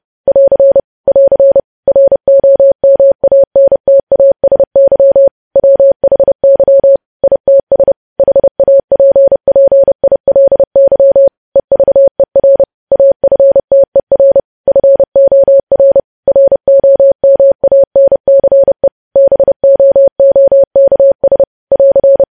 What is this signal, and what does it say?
Morse code, heard as a keyed tone, about 30 wpm, 550 Hz. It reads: <AR> <AR> ROMANTASY WHY ITS HAPPILY EVER AFTER FOR ROMANCE BOOKS <AR>
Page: news headlines in morse code at 30 wpm